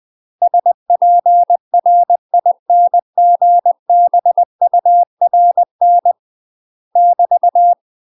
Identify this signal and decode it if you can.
Morse code, heard as a keyed tone, about 20 wpm, 700 Hz. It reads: SPRINGBURN <BT>
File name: UK MASTER QSOs TRAINING / 20WPM / UK MASTER QSOs 0041